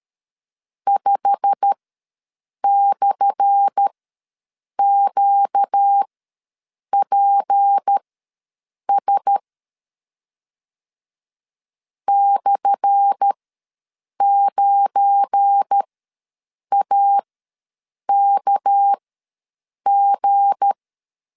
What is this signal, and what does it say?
Morse code, heard as a keyed tone, about 13 wpm, spaced as for 7 wpm, 800 Hz. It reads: 5/QPS /9AKG
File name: LEZIONI / 10 SIGNS / 10A SLASH ws 12wpm 3443